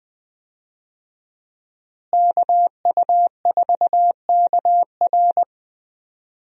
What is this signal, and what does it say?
Morse code, heard as a keyed tone, about 20 wpm, 700 Hz. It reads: KU4KR
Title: USA Call Signs - 5 Characters - Repeated 3x - 20wpm